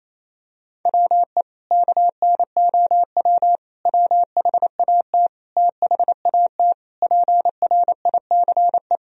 Morse code, heard as a keyed tone, about 28 wpm, 700 Hz. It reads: WE KNOW WHAT THAT PRICE
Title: Sentences from Top 1000 Words - Review Part 4 - 28wpm